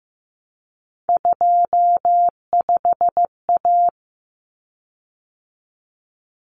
Morse code, heard as a keyed tone, about 15 wpm, 700 Hz. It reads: 25A
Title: Common Cut Numbers - 3-Digit 15wpm